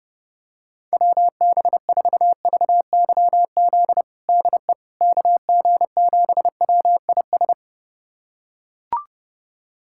Morse code, heard as a keyed tone, about 30 wpm, 700 Hz. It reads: WB4VYZ DE KG7WIS
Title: USA Callsign DE Callsign 30wpm